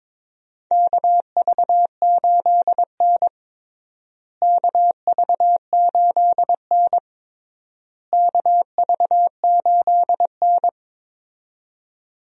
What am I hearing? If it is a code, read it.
Morse code, heard as a keyed tone, about 22 wpm, 700 Hz. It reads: KV8N KV8N KV8N
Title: USA Call Signs - 4 Characters - Repeated 3x - 22wpm